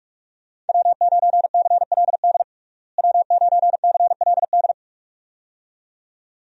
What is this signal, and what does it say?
Morse code, heard as a keyed tone, about 45 wpm, 700 Hz. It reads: W9CLD W9CLD
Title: USA Call Signs - 5 Characters - Repeated 2x - 45wpm